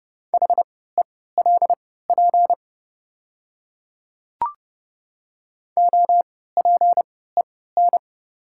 Morse code, heard as a keyed tone, about 30 wpm, spaced as for 17 wpm, 700 Hz. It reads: HELP OPEN
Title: 3x Character Spacing - Top 300 Words - 30wpm